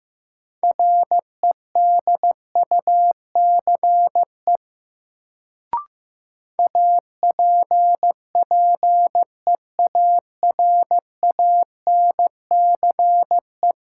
Morse code, heard as a keyed tone, about 16 wpm, 700 Hz. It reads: REDUCE APPEARANCE
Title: Sentences from Top 1100 Words - New Word Drill - 15wpm